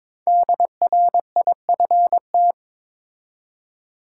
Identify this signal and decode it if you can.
Morse code, heard as a keyed tone, about 22 wpm, 700 Hz. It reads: DRIFT
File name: 5 Letter Words - First 3 Letters Given - 22wpm